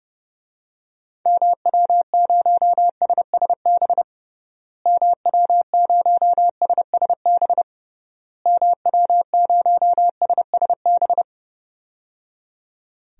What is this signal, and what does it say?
Morse code, heard as a keyed tone, about 30 wpm, 700 Hz. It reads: MW0SSB MW0SSB MW0SSB
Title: DXCC-100 Call Signs - 6 Characters - Repeated 3x - 30wpm